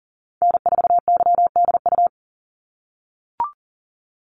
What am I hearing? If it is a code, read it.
Morse code, heard as a keyed tone, about 39 wpm, 700 Hz. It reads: N4YDU
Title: Call Signs - 40wpm